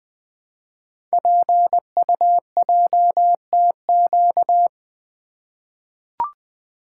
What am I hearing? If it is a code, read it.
Morse code, heard as a keyed tone, about 20 wpm, 700 Hz. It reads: PUJTQ
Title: Code Groups - 5 Letter:Numbers - 20wpm